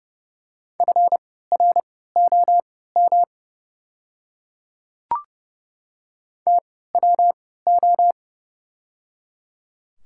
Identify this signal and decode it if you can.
Morse code, heard as a keyed tone, about 30 wpm, spaced as for 17 wpm, 700 Hz. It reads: FROM TWO
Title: 3x Character Spacing - Top 100 Words - 30wpm